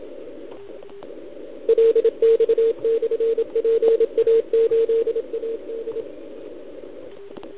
Morse code, heard as a keyed tone, about 28 wpm, 400 Hz. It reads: LX/PA8AD
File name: ARCHIV / OKDX2019 / OKOM40 KX3